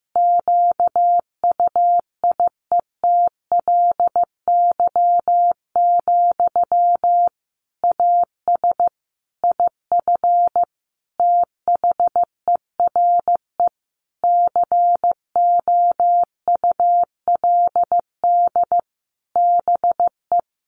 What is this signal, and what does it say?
Morse code, heard as a keyed tone, about 15 wpm, 700 Hz. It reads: QUIETLY, AS IF THERE COULD BE
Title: A Sound Of Thunder 15wpm